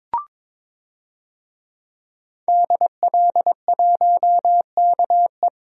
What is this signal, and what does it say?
Morse code, heard as a keyed tone, about 21 wpm, 700 Hz. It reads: DL1KE
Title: DXCC-100 Call Signs - 5 Characters - 22wpm